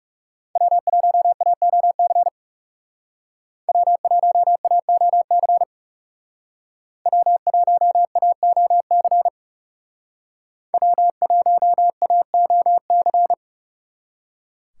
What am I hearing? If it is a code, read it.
Morse code, heard as a keyed tone, about 36 wpm, 700 Hz. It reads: W1AOC W1AOC W1AOC W1AOC
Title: Speed Racing - USA 1x3 Call Signs 30wpm